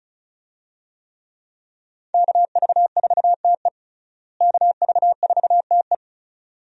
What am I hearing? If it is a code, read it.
Morse code, heard as a keyed tone, about 36 wpm, 700 Hz. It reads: KV4TE KV4TE
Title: USA Call Signs - 5 Characters - Repeated 2x - 35wpm